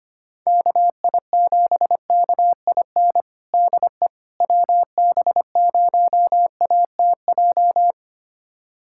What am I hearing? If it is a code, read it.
Morse code, heard as a keyed tone, about 25 wpm, 700 Hz. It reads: KI7KIN DE WB0ATJ
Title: USA Callsign DE Callsign 25wpm